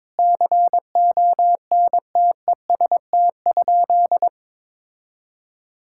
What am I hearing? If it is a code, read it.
Morse code, heard as a keyed tone, about 22 wpm, 700 Hz. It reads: CONTEST?
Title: Sentences from Top 1600 Words 22wpm